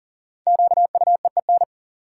Morse code, heard as a keyed tone, about 40 wpm, 700 Hz. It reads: QUEEN
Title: Five Letter Words - 40wpm